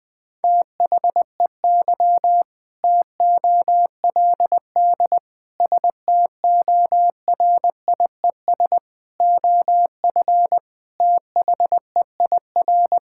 Morse code, heard as a keyed tone, about 19 wpm, 700 Hz. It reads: THEY TOLD STORIES OF THEIR
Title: Sentences with 2-Word Phrases - Spoken Beginning - 20wpm